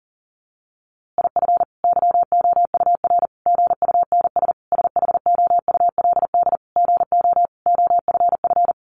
Morse code, heard as a keyed tone, about 40 wpm, 700 Hz. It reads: IF YOUR GUNS SHOULD GO OFF
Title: A Sound Of Thunder 40wpm